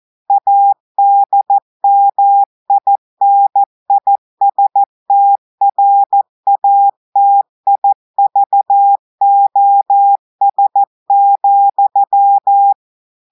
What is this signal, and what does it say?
Morse code, heard as a keyed tone, about 14 wpm, 800 Hz. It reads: ADMINISTRATIVOS,
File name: audios / portugais / portugais texte en clair 2 vitesse 14 mots minutes